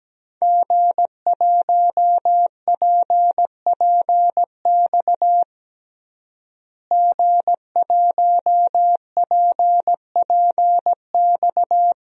Morse code, heard as a keyed tone, about 17 wpm, 700 Hz. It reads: G1PPX G1PPX
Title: DXCC-100 Call Signs - 5 Characters - Repeated 2x - 17wpm